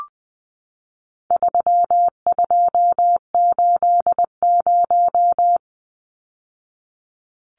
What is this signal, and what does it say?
Morse code, heard as a keyed tone, about 20 wpm, 700 Hz. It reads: 3280
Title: Numbers - 4-Digit 20wpm